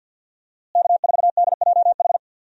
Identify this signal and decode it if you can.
Morse code, heard as a keyed tone, about 46 wpm, 700 Hz. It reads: K4DJH
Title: USA Call Signs - 5 Characters - Repeated 2x - 50wpm